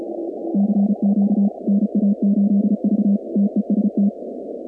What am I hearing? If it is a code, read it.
Morse code, heard as a keyed tone, about 34 wpm, 200 Hz. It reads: CQ NA8V TEST
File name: contests / 2019-cw-wpx-kt5j-k5pi- -2330 0-46